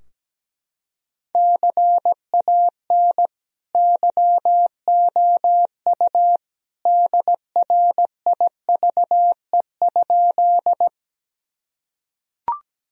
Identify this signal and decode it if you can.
Morse code, heard as a keyed tone, about 17 wpm, 700 Hz. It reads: CAN YOU DRIVE?
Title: Sentences from Top 1000 Words - Review Part 2 - 17wpm